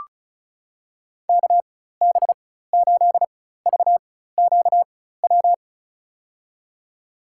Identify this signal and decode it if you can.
Morse code, heard as a keyed tone, about 36 wpm, spaced as for 16 wpm, 700 Hz. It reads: KB8VQW
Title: USA Call Signs - 4x Character Spacing - 35wpm